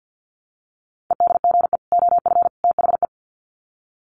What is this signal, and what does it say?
Morse code, heard as a keyed tone, about 48 wpm, 700 Hz. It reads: EDGE OF THE
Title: Sets of 3 Words 50wpm